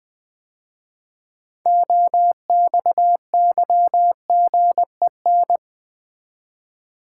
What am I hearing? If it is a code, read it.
Morse code, heard as a keyed tone, about 20 wpm, 700 Hz. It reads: OXYGEN
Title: 6 Letter Words - First 3 Letters Given - 20wpm